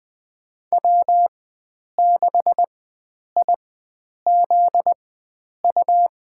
Morse code, heard as a keyed tone, about 20 wpm, spaced as for 9 wpm, 700 Hz. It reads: W6IZU
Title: USA Call Signs - 5 Characters - 4x Character Spacing - 20wpm